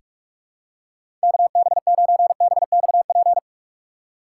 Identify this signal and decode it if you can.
Morse code, heard as a keyed tone, about 44 wpm, 700 Hz. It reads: KB9BXP
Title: USA Call Signs - 6 Characters - 2x Character Spacing - 45wpm